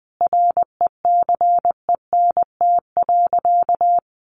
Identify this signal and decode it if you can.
Morse code, heard as a keyed tone, about 20 wpm, 700 Hz. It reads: RECENT.
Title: Sentences from Top 1400 Words 20wpm